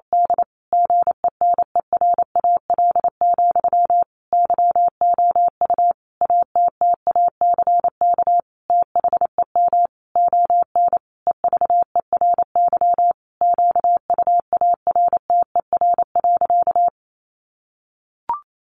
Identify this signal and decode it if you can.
Morse code, heard as a keyed tone, about 28 wpm, 700 Hz. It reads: D GENERAL, YOU ATTACK THEM ON EVERY QUARTER.
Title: Sentences from Top 1500 Words 28wpm